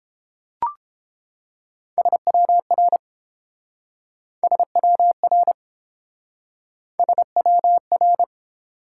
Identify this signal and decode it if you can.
Morse code, heard as a keyed tone, about 32 wpm, 700 Hz. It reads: SWR SWR SWR